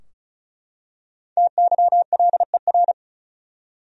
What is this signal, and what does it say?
Morse code, heard as a keyed tone, about 34 wpm, 700 Hz. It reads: TYLER
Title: USA Common Names - Max 5 Letters - 35wpm